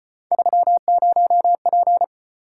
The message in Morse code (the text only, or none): 30P